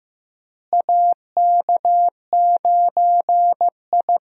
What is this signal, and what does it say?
Morse code, heard as a keyed tone, about 15 wpm, 700 Hz. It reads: AK9I
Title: USA Call Signs - 4 Characters - Repeated 3x - 15wpm